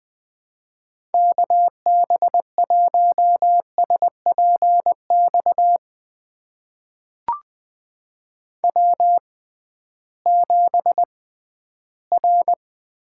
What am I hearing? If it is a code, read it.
Morse code, heard as a keyed tone, about 20 wpm, spaced as for 13 wpm, 700 Hz. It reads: KB1SPX W7R
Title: USA Call Signs - 6x Character Spacing - 20wpm